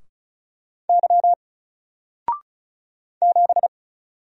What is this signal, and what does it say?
Morse code, heard as a keyed tone, about 35 wpm, 700 Hz. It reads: Y 7
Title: Single Letter-Number Alt 35wpm